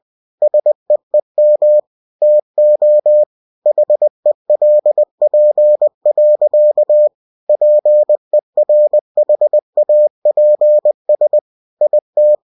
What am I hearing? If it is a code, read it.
Morse code, heard as a keyed tone, about 20 wpm, 600 Hz. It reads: SEEM TO HELP. PERHAPS IT